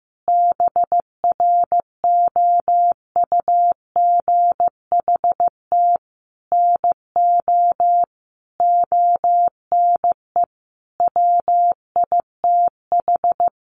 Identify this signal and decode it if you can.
Morse code, heard as a keyed tone, about 15 wpm, 700 Hz. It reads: BROUGHT NO ONE WITH
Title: Sentences from Top 500 Words 15wpm